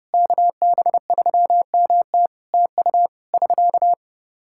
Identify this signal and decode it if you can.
Morse code, heard as a keyed tone, about 30 wpm, 700 Hz. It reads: KB3MT TU <SK>
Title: K1USN SST QSO Practice 30wpm